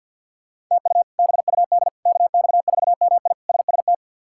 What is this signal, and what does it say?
Morse code, heard as a keyed tone, about 47 wpm, 700 Hz. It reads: TU BUD KX4MI SST
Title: K1USN SST QSO Practice - 50wpm